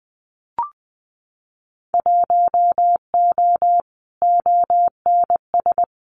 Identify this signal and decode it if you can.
Morse code, heard as a keyed tone, about 20 wpm, 700 Hz. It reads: 1O ONS
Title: Winter Field Day Exchanges 20wpm